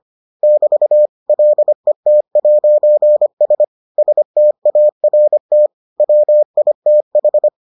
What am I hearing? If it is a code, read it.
Morse code, heard as a keyed tone, about 25 wpm, 600 Hz. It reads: <BT> LET'S START WITH